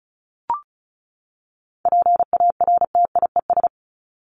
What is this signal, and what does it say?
Morse code, heard as a keyed tone, about 34 wpm, 700 Hz. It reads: PARTIES
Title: ES:S Suffix - Top 100 Words with Base Words 35wpm